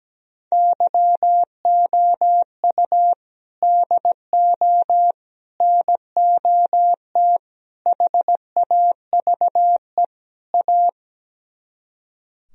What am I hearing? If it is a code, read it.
Morse code, heard as a keyed tone, about 17 wpm, 700 Hz. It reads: YOU DO NOT HAVE A
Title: Speed-Racing - Sets of 5 Words 17wpm